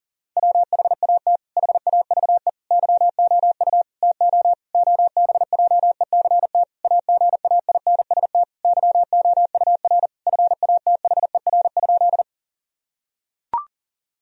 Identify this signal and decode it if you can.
Morse code, heard as a keyed tone, about 40 wpm, 700 Hz. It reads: WHAT HAVE YOU TO OBJECT AGAINST YOUR FATHER?
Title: Sentences from Top 1000 Words - Review Part 3 - 40wpm